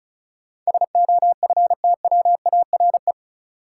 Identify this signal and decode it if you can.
Morse code, heard as a keyed tone, about 36 wpm, 700 Hz. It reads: SOFTWARE
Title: Sentences from Top 1000 Words - Review Part 3 - New Word Drill - 35wpm